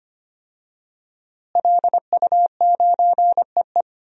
Morse code, heard as a keyed tone, about 25 wpm, 700 Hz. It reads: LU9EE
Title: DXCC-100 Call Signs - 5 Characters - Repeated 2x - 25wpm